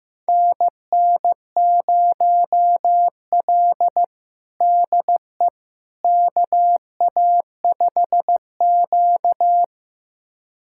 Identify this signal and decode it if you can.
Morse code, heard as a keyed tone, about 15 wpm, 700 Hz. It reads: NN0L DE KA5Q
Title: USA Callsign DE Callsign - 4 Characters - 15wpm